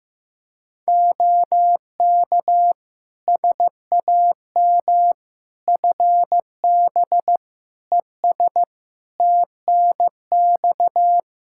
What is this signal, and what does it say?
Morse code, heard as a keyed tone, about 15 wpm, 700 Hz. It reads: OK SAM FB ES TNX
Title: Practice QSO - Exchange 2 - Info - 15wpm